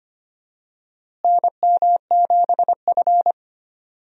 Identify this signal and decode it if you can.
Morse code, heard as a keyed tone, about 25 wpm, 700 Hz. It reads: NM7F
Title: USA Call Signs - 4 Characters - Repeated 3x - 25wpm